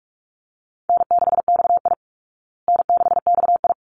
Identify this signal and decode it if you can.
Morse code, heard as a keyed tone, about 44 wpm, 700 Hz. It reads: N6XI N6XI
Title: Easy Call Signs Repeated 2x - 45wpm